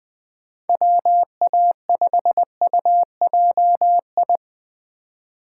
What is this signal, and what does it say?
Morse code, heard as a keyed tone, about 20 wpm, 700 Hz. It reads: WA5UJI
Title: USA Call Signs - 3x Character Spacing - 20wpm